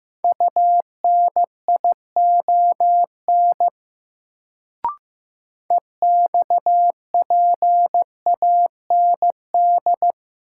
Morse code, heard as a keyed tone, about 15 wpm, 700 Hz. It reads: UNION EXPAND
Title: Sentences from Top 1400 Words - New Word Drill - 15wpm